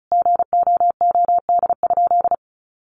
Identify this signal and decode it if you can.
Morse code, heard as a keyed tone, about 34 wpm, 700 Hz. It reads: GOOD?
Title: Sentences from Top 1200 Words 35wpm